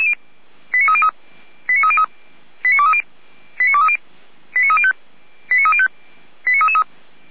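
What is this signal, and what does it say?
Five-tone selective call (ZVEI), heard as quick runs of five tones, tones of about 70 ms. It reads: 89303 89303 89230 89230 89305 89305 89333
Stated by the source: ZVEI five-tone selective call received by SDR, from the Feuerwehr Orlamünde (fire brigade) audio server.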